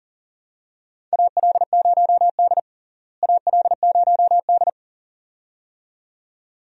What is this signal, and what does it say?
Morse code, heard as a keyed tone, about 40 wpm, 700 Hz. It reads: AL0D AL0D
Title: USA Call Signs - 4 Characters - Repeated 2x - 40wpm